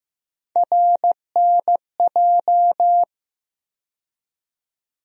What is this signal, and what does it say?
Morse code, heard as a keyed tone, about 15 wpm, 700 Hz. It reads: RNJ
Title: USA Call Sign Postfixes - 3 Letters 15wpm